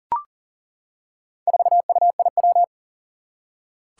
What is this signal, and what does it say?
Morse code, heard as a keyed tone, about 39 wpm, 700 Hz. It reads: 4UIW
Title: Four Challenging Letter-Number Combinations 40wpm